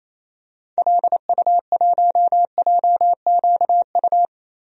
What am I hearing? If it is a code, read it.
Morse code, heard as a keyed tone, about 28 wpm, 700 Hz. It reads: LU1JQU
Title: DXCC-100 Call Signs - 6 Characters - Repeated 3x - 28wpm